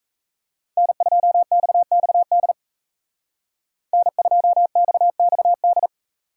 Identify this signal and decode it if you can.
Morse code, heard as a keyed tone, about 36 wpm, 700 Hz. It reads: N2XXD N2XXD